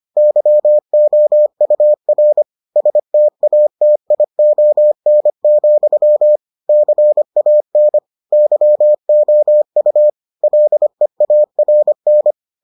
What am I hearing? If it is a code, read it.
Morse code, heard as a keyed tone, about 25 wpm, 600 Hz. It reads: YOUR STATION, CAN YOU LEARN